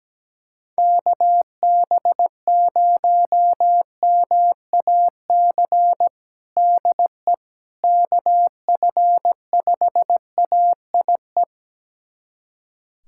Morse code, heard as a keyed tone, about 17 wpm, 700 Hz. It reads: KB0MAC DE KF5AIE
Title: USA Callsign DE Callsign - 6 Characters - 17wpm